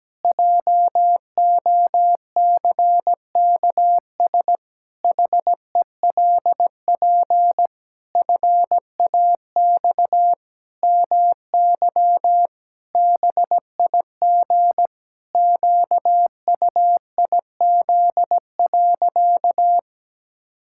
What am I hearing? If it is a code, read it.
Morse code, heard as a keyed tone, about 17 wpm, 700 Hz. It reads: JOCKS HELP FAX MY BIG QUIZ.
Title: Pangram Challenge v2 - 17wpm